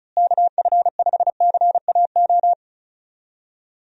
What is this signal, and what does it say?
Morse code, heard as a keyed tone, about 34 wpm, 700 Hz. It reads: KF5CAO
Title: USA Call Signs - 6 Characters - Repeated 3x - 35wpm